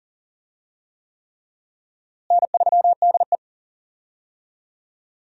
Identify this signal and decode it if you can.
Morse code, heard as a keyed tone, about 37 wpm, 700 Hz. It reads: N3DE
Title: USA Call Signs - 4 Characters - Repeated 3x - 40wpm